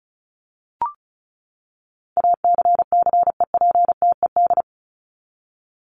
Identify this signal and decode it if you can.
Morse code, heard as a keyed tone, about 34 wpm, 700 Hz. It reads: ACCEPTED
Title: ED Suffix - Top 100 Words with Base Words 35wpm